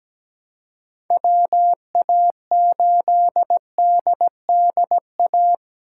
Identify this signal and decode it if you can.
Morse code, heard as a keyed tone, about 17 wpm, 700 Hz. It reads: WA8DDA
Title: USA Call Signs - 6 Characters - 3x Character Spacing - 17wpm